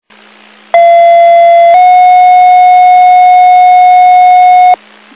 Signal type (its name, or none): two-tone page